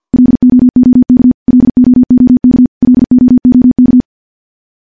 Morse code, heard as a keyed tone, about 50 wpm, 250 Hz. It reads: LOOK LOOK LOOK
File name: morse / morse-training-50WPM